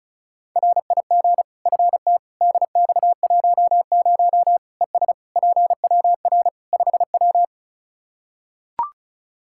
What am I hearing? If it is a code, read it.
Morse code, heard as a keyed tone, about 34 wpm, 700 Hz. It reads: RIG FT DX10 ES PWR 5W
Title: Practice QSO - Exchange 2 - Info - 35wpm